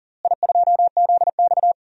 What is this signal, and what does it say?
Morse code, heard as a keyed tone, about 37 wpm, 700 Hz. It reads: I2ZX
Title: DXCC-100 Call Signs - 4 Characters - Repeated 2x - 40wpm